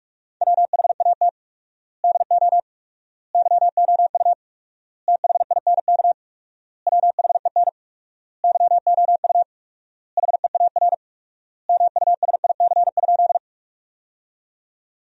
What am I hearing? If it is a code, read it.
Morse code, heard as a keyed tone, about 46 wpm, 700 Hz. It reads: WHAT DO YOU THINK WHEN YOU HEAR MUSIC?
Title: 4x Word-Spacing - Sentences from Top 300 Words - 45wpm